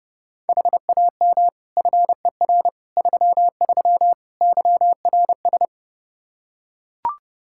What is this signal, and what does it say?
Morse code, heard as a keyed tone, about 32 wpm, 700 Hz. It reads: HAM FER 33 YRS
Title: Practice QSO - Exchange 3 - More Info - 30wpm